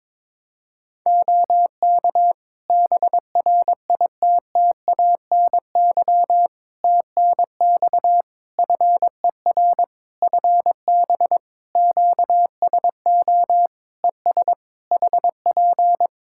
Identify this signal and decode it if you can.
Morse code, heard as a keyed tone, about 22 wpm, 700 Hz. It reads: OK BRITTANY TNX FER FB QSO ES HP
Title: Practice QSO - Exchange 4 - Ending - 22wpm